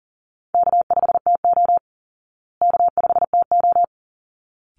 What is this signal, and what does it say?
Morse code, heard as a keyed tone, about 40 wpm, 700 Hz. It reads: K5TO K5TO
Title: Call Signs Repeated 3x - 40wpm